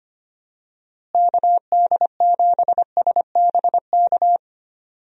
Morse code, heard as a keyed tone, about 25 wpm, 700 Hz. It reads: KD7SBK
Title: USA Call Signs - 6 Characters - Repeated 3x - 25wpm